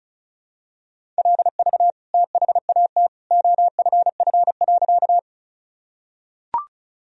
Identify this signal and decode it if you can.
Morse code, heard as a keyed tone, about 36 wpm, 700 Hz. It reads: LV THAT OFF.
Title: Sentences from Top 400 Words and Abbreviations 35wpm